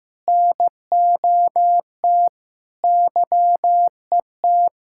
Morse code, heard as a keyed tone, about 15 wpm, 700 Hz. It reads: NOT YET
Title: Sentences from Top 1000 Words - Review Part 5 - 15wpm